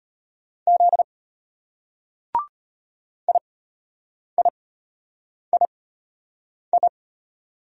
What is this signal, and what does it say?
Morse code, heard as a keyed tone, about 33 wpm, spaced as for 8 wpm, 700 Hz. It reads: Z IIII